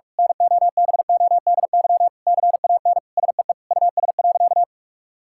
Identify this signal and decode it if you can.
Morse code, heard as a keyed tone, about 46 wpm, 700 Hz. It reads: NOBODY CAN SEE US.
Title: Sentences from Top 1600 Words 45wpm